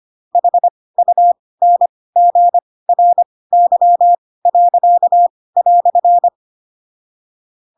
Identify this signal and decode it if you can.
Morse code, heard as a keyed tone, about 25 wpm, spaced as for 18 wpm, 700 Hz. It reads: HUNGRY."
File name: Pinocchio / Pinocchio ch6 12 25 ik1hge